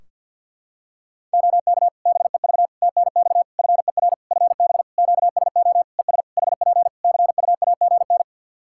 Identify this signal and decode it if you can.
Morse code, heard as a keyed tone, about 49 wpm, 700 Hz. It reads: OK BEV TNX FER FB QSO ES HP CUAGN
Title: Practice QSO - Exchange 4 - Ending - 50wpm